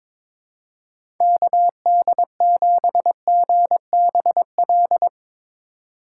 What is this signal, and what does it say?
Morse code, heard as a keyed tone, about 21 wpm, 700 Hz. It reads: KD7GBL
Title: USA Call Signs - 6 Characters - 4x Character Spacing - 22wpm